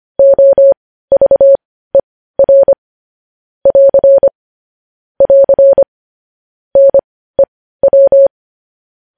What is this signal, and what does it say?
Morse code, heard as a keyed tone, about 25 wpm, spaced as for 15 wpm, 550 Hz. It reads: OVER <AR> <AR> NEW
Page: news-headlines-in-morse-code-at-15-wpm